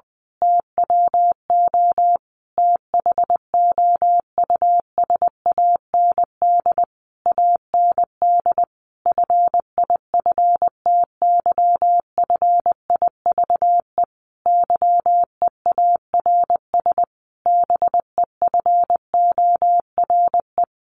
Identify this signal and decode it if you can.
Morse code, heard as a keyed tone, about 20 wpm, 700 Hz. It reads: TWO THOUSAND AND FIFTYFIVE YEARS BEFORE
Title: A Sound Of Thunder 20wpm